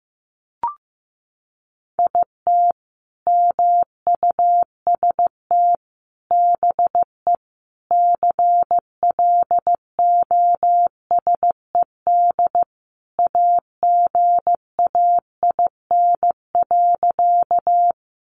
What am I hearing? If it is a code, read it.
Morse code, heard as a keyed tone, about 15 wpm, 700 Hz. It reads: IT MUST BE CLOSED AGAIN.
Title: Sentences from Top 1400 Words 15wpm